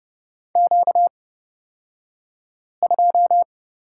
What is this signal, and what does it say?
Morse code, heard as a keyed tone, about 30 wpm, spaced as for 5 wpm, 700 Hz. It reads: Q2
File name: Lesson 38d - Letter q - Two Characters 30wpm